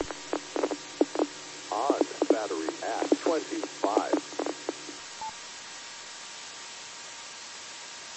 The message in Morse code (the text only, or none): none